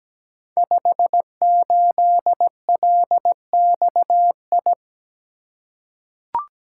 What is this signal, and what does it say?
Morse code, heard as a keyed tone, about 17 wpm, 690 Hz. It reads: 58LXI
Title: Code Groups - 5 Letter:Numbers - 17wpm